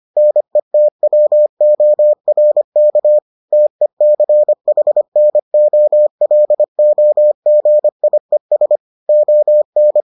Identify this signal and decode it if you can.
Morse code, heard as a keyed tone, about 25 wpm, 600 Hz. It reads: NETWORK TECHNOLOGIES ON